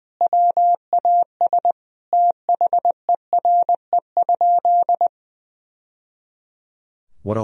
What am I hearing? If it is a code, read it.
Morse code, heard as a keyed tone, about 20 wpm, 700 Hz. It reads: WAS THERE?
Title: Sentences from Top 1000 Words - Review Part 4 - 20wpm